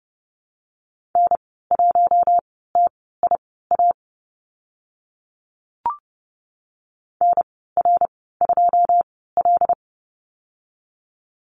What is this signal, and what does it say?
Morse code, heard as a keyed tone, about 30 wpm, spaced as for 17 wpm, 700 Hz. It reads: N1TIA NR2L
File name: Lesson 14g - Number 2 - Call Signs 30wpm